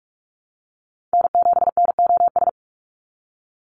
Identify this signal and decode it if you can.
Morse code, heard as a keyed tone, about 46 wpm, 690 Hz. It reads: N7NOS
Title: USA Call Signs 45wpm